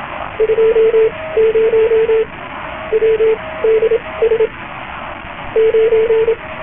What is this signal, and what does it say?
Morse code, heard as a keyed tone, about 27 wpm, spaced as for 19 wpm, 450 Hz, with radio noise. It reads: 20 WDS 9